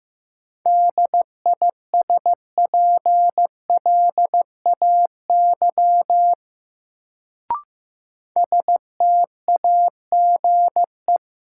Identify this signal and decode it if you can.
Morse code, heard as a keyed tone, about 15 wpm, 700 Hz. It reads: DISPLAY STAGE
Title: Sentences from Top 1200 Words - New Word Drill Alt - 15wpm